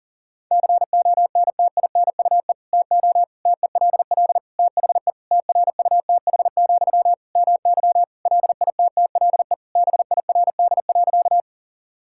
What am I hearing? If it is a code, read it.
Morse code, heard as a keyed tone, about 37 wpm, 700 Hz. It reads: CONTINUE TO TELL THE TRUTH, MY LITTLE BIRD.
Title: Sentences from Top 1000 Words - Review Part 6 - 40wpm